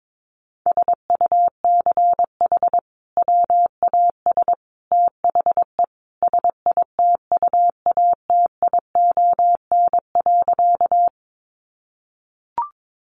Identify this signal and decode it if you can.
Morse code, heard as a keyed tone, about 22 wpm, 700 Hz. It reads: SUCH WAS THE SITUATION.
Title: Sentences from Top 1200 Words 22wpm